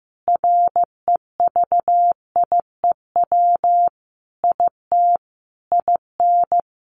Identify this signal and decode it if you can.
Morse code, heard as a keyed tone, about 15 wpm, 700 Hz. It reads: REVIEW IT IN
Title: Sentences from Top 1500 Words 15wpm